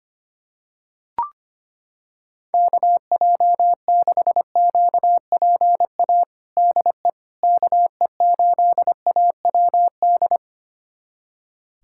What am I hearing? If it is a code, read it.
Morse code, heard as a keyed tone, about 25 wpm, 700 Hz. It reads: KJ6QPA DE KE8AWD
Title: USA Callsign DE Callsign - 6 Characters - 25wpm